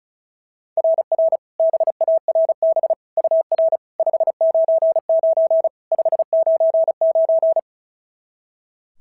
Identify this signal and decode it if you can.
Morse code, heard as a keyed tone, about 36 wpm, 690 Hz. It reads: RR BARB UR 599 599
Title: SOTA QSO Practice 35wpm